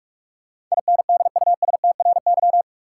Morse code, heard as a keyed tone, about 46 wpm, 700 Hz. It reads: INDUSTRY
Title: Sentences from Top 1000 Words - Review Part 4 - New Word Drill - 45wpm